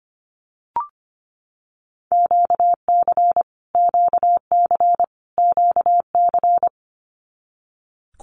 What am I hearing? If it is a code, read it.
Morse code, heard as a keyed tone, about 25 wpm, 700 Hz. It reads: QC QC QC
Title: Canadian Provinces and Territories - Repeated 3x - 25wpm